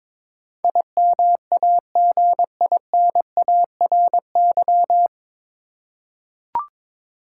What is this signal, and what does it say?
Morse code, heard as a keyed tone, about 22 wpm, 700 Hz. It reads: IMAGINARY
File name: 9 Letter Words - First 3 Letters Given - 22wpm